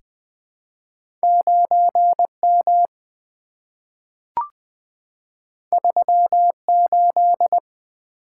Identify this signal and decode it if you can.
Morse code, heard as a keyed tone, about 20 wpm, 700 Hz. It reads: 9 M 3 8
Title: Two Letter-Number Combinations - 20wpm